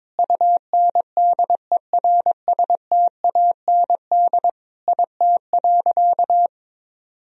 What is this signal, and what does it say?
Morse code, heard as a keyed tone, about 22 wpm, 700 Hz. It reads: UNDERSTAND IT.
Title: Sentences from Top 1000 Words - Review Part 4 - 22wpm